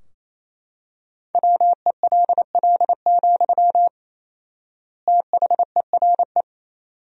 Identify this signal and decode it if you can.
Morse code, heard as a keyed tone, about 28 wpm, 700 Hz. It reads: WELL, THERE
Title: 4x Word-Spacing - Sentences from Top 500 Words - 28wpm